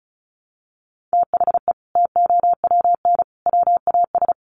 Morse code, heard as a keyed tone, about 34 wpm, 700 Hz. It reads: THE TOWN WAS
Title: Sentences from Top 1300 Words 35wpm